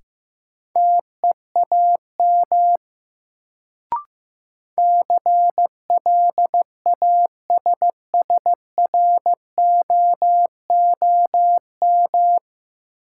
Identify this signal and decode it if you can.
Morse code, heard as a keyed tone, about 15 wpm, 700 Hz. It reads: TEAM CLASSROOM
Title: Sentences from Top 1500 Words - New Word Drill - 15wpm